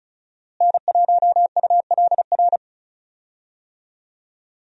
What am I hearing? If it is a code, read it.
Morse code, heard as a keyed tone, about 36 wpm, 700 Hz. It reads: N1ULR
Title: USA Call Signs - 5 Characters - Repeated 2x - 35wpm